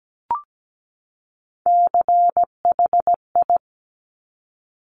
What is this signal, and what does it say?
Morse code, heard as a keyed tone, about 17 wpm, 700 Hz. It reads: CHI
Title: Top Three Letter Combinations in Words 17wpm